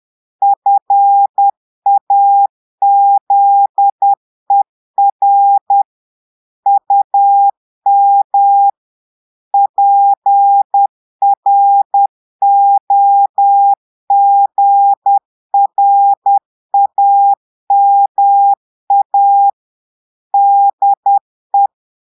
Morse code, heard as a keyed tone, about 10 wpm, 800 Hz. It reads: FAZER UM PROGRAMA DE
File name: portugais texte en clair 2 vitesse 10 mots minutes